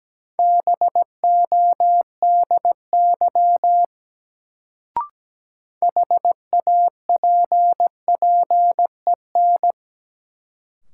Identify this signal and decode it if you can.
Morse code, heard as a keyed tone, about 17 wpm, 700 Hz. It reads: BODY HAPPEN
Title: Sentences from Top 400 Words - New Word Drill - 17wpm